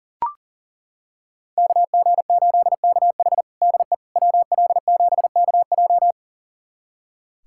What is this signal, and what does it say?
Morse code, heard as a keyed tone, about 40 wpm, 700 Hz. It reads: KG8KH DE WL7KJ
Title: USA Callsign DE Callsign - 5 Characters - 40wpm